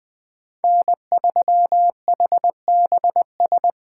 Morse code, heard as a keyed tone, about 20 wpm, 700 Hz. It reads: N3HBS
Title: Speed Racing - USA 1x3 Call Signs 17wpm